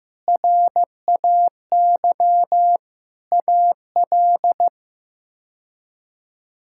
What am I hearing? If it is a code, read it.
Morse code, heard as a keyed tone, about 15 wpm, 700 Hz. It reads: RAY AL
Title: K1USN SST QSO Practice - 15wpm